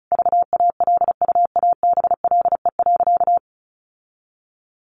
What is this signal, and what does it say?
Morse code, heard as a keyed tone, about 34 wpm, 700 Hz. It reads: VALUABLE.
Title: Sentences from Top 1100 Words 35wpm